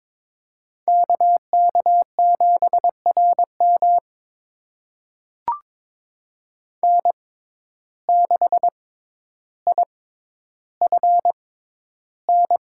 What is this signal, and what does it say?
Morse code, heard as a keyed tone, about 22 wpm, spaced as for 11 wpm, 700 Hz. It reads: KK7RM N6IFN
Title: USA Call Signs - 5 Characters - 6x Character Spacing - 22wpm